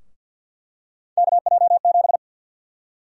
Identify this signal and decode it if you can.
Morse code, heard as a keyed tone, about 49 wpm, 700 Hz. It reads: KJ7
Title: USA Call Sign Prefixes - Repeated 2x - 50wpm